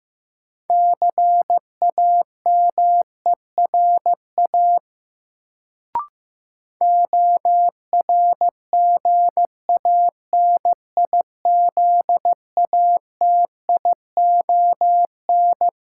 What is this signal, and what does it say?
Morse code, heard as a keyed tone, about 15 wpm, 700 Hz. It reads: CAMERA ORGANIZATION
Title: Sentences from Top 1000 Words - Review Part 5 - New Word Drill - 15wpm